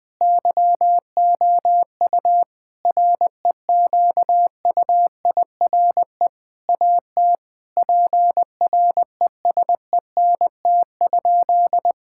Morse code, heard as a keyed tone, about 20 wpm, 700 Hz. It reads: YOU REQUIRE AT PRESENT?
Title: Sentences from Top 1000 Words - Review Part 6 - 20wpm